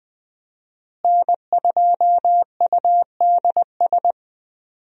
Morse code, heard as a keyed tone, about 20 wpm, 700 Hz. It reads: N2UDS
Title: USA Call Signs - 5 Characters - Repeated 2x - 20wpm